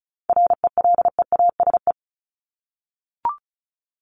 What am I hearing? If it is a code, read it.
Morse code, heard as a keyed tone, about 34 wpm, 700 Hz. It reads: RELEASE
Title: Top 1001-1250 Words 35wpm